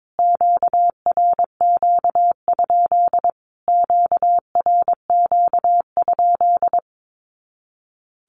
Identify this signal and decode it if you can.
Morse code, heard as a keyed tone, about 22 wpm, 700 Hz. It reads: QRQ? QRQ?
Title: Q-Codes Repeated 3x 22wpm